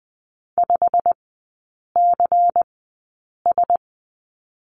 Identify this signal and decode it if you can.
Morse code, heard as a keyed tone, about 20 wpm, spaced as for 8 wpm, 700 Hz. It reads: 5CS
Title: 152 Lesson 25e - Letter p - Three Characters 20wpm